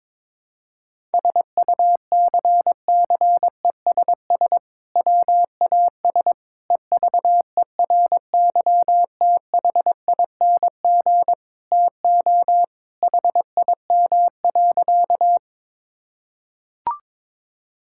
Morse code, heard as a keyed tone, about 21 wpm, 700 Hz. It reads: SUCCESS WAS EVERYTHING TO HIM.
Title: Sentences from Top 1000 Words - Review Part 6 - 22wpm